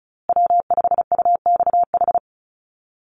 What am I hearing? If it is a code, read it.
Morse code, heard as a keyed tone, about 34 wpm, 700 Hz. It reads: W5UXH
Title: Call Signs Repeated 2x - 35wpm